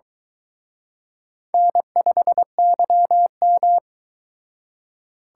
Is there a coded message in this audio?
Yes: Morse code from 1.5 to 3.8 s, heard as a keyed tone, about 23 wpm, 700 Hz. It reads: N5YM